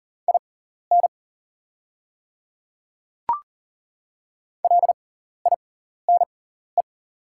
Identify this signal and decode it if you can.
Morse code, heard as a keyed tone, about 40 wpm, spaced as for 14 wpm, 700 Hz. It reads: IN LINE
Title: 6x Character Spacing - Top 200 Words - 40wpm